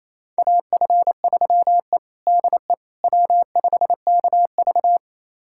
Morse code, heard as a keyed tone, about 28 wpm, 700 Hz. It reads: AF3E DE W5KV
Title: USA Callsign DE Callsign - 4 Characters - 28wpm